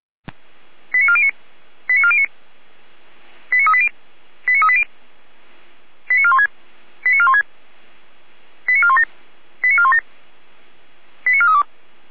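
Five-tone selective call (ZVEI), heard as quick runs of five tones, tones of about 70 ms. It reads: 89409 89409 89380 89380 89416 89416 89417 89417 89432
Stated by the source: ZVEI five-tone selective call received by SDR.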